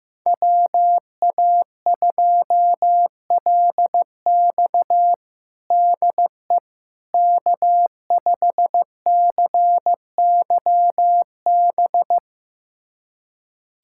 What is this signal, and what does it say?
Morse code, heard as a keyed tone, about 16 wpm, 700 Hz. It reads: WA2LX DE K5CYB
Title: USA Callsign DE Callsign - 5 Characters - 15wpm